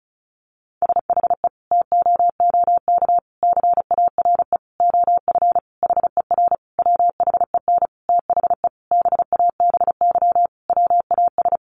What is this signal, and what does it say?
Morse code, heard as a keyed tone, about 34 wpm, 700 Hz. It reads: SHE TOOK CARE OF HER WHEN THE BABY WAS
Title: Sentences from Top 1300 Words 35wpm